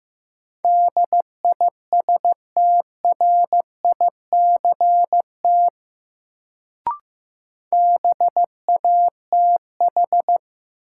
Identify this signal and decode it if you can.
Morse code, heard as a keyed tone, about 15 wpm, 700 Hz. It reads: DISTRICT BATH
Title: Sentences from Top 1600 Words - New Word Drill - 15wpm